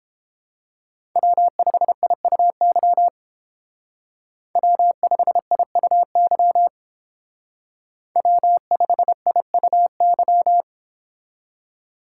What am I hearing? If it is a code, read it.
Morse code, heard as a keyed tone, about 29 wpm, 700 Hz. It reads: W5IUY W5IUY W5IUY